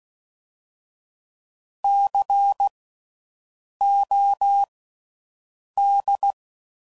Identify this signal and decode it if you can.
Morse code, heard as a keyed tone, about 16 wpm, spaced as for 6 wpm, 800 Hz. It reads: COD
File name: Slow With Faster Ings